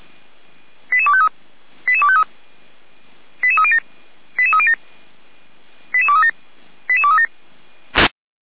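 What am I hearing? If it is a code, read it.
Five-tone selective call (ZVEI), heard as quick runs of five tones, tones of about 70 ms. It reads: 88253 88253 88307 88307 88237 88237
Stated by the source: ZVEI five-tone selective call received by SDR.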